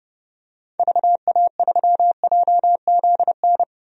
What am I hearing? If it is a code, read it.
Morse code, heard as a keyed tone, about 29 wpm, 700 Hz. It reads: VA3JZN
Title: DXCC-100 Call Signs - 6 Characters - 30wpm